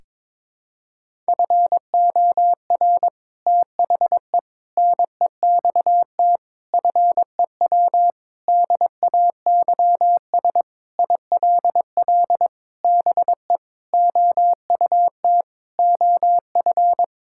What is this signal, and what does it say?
Morse code, heard as a keyed tone, about 22 wpm, 700 Hz. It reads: FOR THE NEXT FEW DAYS ILL BE OUT OF
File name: practice / 22wpm / Sentences with 3-Word Phrases - 22wpm